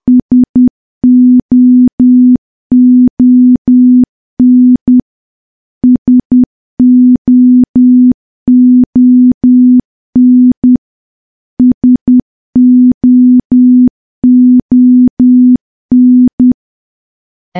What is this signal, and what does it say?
Morse code, heard as a keyed tone, about 10 wpm, 250 Hz. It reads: SOON SOON SOON
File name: morse / morse-training-10WPM